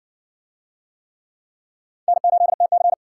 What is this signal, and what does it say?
Morse code, heard as a keyed tone, about 60 wpm, 700 Hz. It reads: N8TC